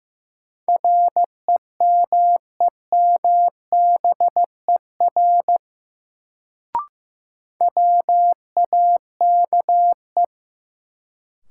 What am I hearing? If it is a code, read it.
Morse code, heard as a keyed tone, about 15 wpm, 700 Hz. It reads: REMEMBER WAKE
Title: Sets of 4 Words - New Word Drill - 15wpm